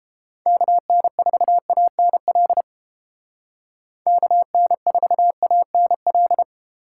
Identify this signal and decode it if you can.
Morse code, heard as a keyed tone, about 31 wpm, 700 Hz. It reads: KN4ANL KN4ANL